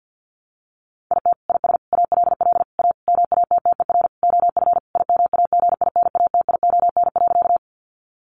Morse code, heard as a keyed tone, about 48 wpm, 700 Hz. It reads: IT IS ALL A MATTER OF IMAGINATION.
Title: Sentences from Top 1300 Words 50wpm